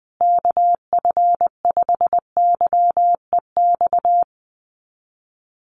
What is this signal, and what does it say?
Morse code, heard as a keyed tone, about 20 wpm, 700 Hz. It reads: KF5YEX
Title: USA Call Signs Repeated 3x 20wpm